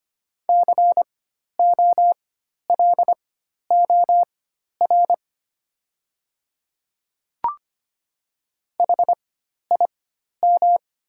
Morse code, heard as a keyed tone, about 25 wpm, spaced as for 12 wpm, 700 Hz. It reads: COLOR HIM
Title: 4x Character Spacing - Top 500 Words - 25wpm